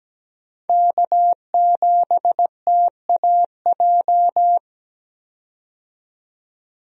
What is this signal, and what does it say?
Morse code, heard as a keyed tone, about 17 wpm, 700 Hz. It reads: K7TAJ
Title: USA Call Signs - 5 Characters - 17wpm